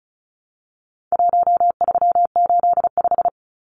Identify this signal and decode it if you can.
Morse code, heard as a keyed tone, about 34 wpm, 700 Hz. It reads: 1385
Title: Numbers - 4-Digit 35wpm